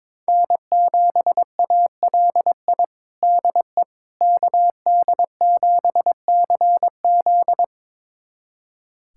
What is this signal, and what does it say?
Morse code, heard as a keyed tone, about 22 wpm, 700 Hz. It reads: N7ALI DE KD7CZ
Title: USA Callsign DE Callsign - 5 Characters - 22wpm